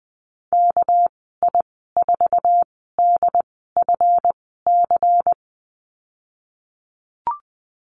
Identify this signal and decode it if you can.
Morse code, heard as a keyed tone, about 20 wpm, spaced as for 15 wpm, 700 Hz. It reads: KI4DFC
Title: 231 Lesson 37g - Letter x - Call Signs 20wpm